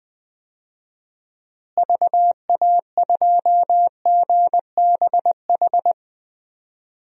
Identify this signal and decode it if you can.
Morse code, heard as a keyed tone, about 20 wpm, 700 Hz. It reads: VA2GBH
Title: DXCC-100 Call Signs - 6 Characters - Repeated 3x - 20wpm